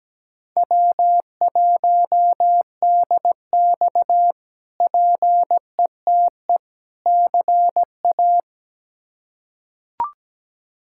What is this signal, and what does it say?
Morse code, heard as a keyed tone, about 17 wpm, 700 Hz. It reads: W1DX PETE CA
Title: K1USN SST QSO Practice - 17wpm